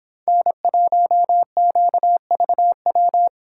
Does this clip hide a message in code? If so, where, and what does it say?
Yes: Morse code from 0.3 to 3.3 s, heard as a keyed tone, about 26 wpm, 700 Hz. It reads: N1QVW